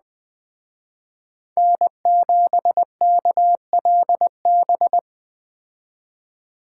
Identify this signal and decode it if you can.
Morse code, heard as a keyed tone, about 20 wpm, 700 Hz. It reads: N7KLB
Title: Speed Racing - USA 1x3 Call Signs 17wpm